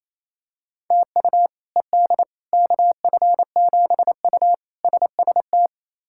Morse code, heard as a keyed tone, about 28 wpm, 700 Hz. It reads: TU ED KF7U SST
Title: K1USN SST QSO Practice - 28wpm